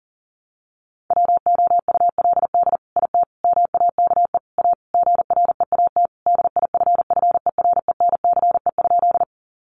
Morse code, heard as a keyed tone, about 40 wpm, 700 Hz. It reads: WOULD IT MAKE A GREAT DIFFERENCE?
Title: Sentences from Top 800 Words 40wpm